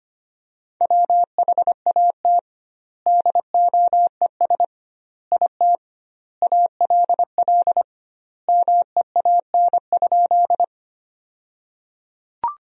Morse code, heard as a keyed tone, about 25 wpm, 700 Hz. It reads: WHAT DOES IT ALL MEAN?
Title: 2x Word-Spacing - Sentences from Top 200 Words - 25wpm